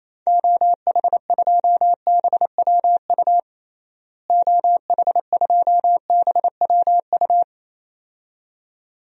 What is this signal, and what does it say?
Morse code, heard as a keyed tone, about 28 wpm, 700 Hz. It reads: OH2BWU OH2BWU
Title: DXCC-100 Call Signs - 6 Characters - Repeated 2x - 28wpm